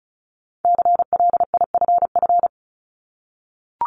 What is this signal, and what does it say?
Morse code, heard as a keyed tone, about 36 wpm, 700 Hz. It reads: CLIFF
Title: Multi-letter Phonemes Repeated 3x 35wpm